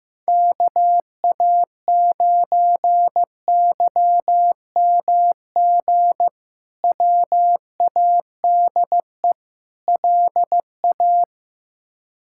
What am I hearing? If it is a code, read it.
Morse code, heard as a keyed tone, about 15 wpm, 700 Hz. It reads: KA9YMG WADE LA
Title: K1USN SST QSO Practice - 15wpm